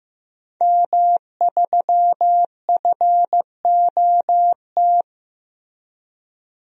Morse code, heard as a keyed tone, about 15 wpm, 700 Hz. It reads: M3FOT
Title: DXCC-100 Call Signs - 5 Characters - Repeated 3x - 15wpm